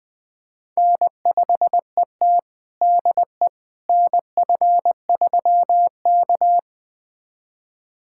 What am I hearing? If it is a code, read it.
Morse code, heard as a keyed tone, about 20 wpm, 700 Hz. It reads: N5ET DE NF3K
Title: USA Callsign DE Callsign - 4 Characters - 20wpm